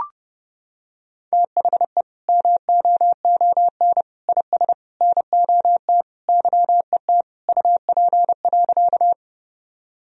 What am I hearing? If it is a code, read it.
Morse code, heard as a keyed tone, about 30 wpm, 700 Hz. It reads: THE MOON IS NOT YET UP.
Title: Sentences from Top 1000 Words - Review Part 3 - 30wpm